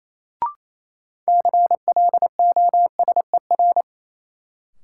Morse code, heard as a keyed tone, about 28 wpm, 700 Hz. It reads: CLOSER
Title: Sets of 5 Words Encore - New Word Drill - 28wpm